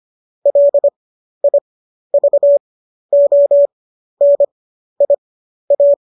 Morse code, heard as a keyed tone, about 25 wpm, spaced as for 12 wpm, 570 Hz. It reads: LIVONIA